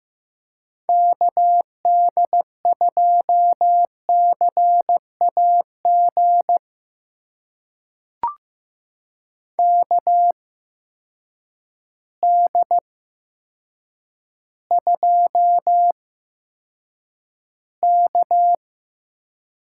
Morse code, heard as a keyed tone, about 15 wpm, spaced as for 7 wpm, 700 Hz. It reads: KD2CAG KD2K
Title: USA Call Signs - 6 Characters - 8x Character Spacing - 15wpm